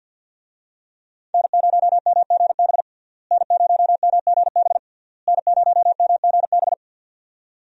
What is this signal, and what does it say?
Morse code, heard as a keyed tone, about 53 wpm, 700 Hz. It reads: N0MGB N0MGB N0MGB
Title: USA Call Signs - 5 Characters - Repeated 3x - 50wpm